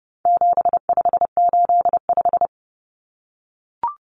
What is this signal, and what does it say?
Morse code, heard as a keyed tone, about 30 wpm, 700 Hz. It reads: 7585
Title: Common Cut Numbers - 4-Digit 30wpm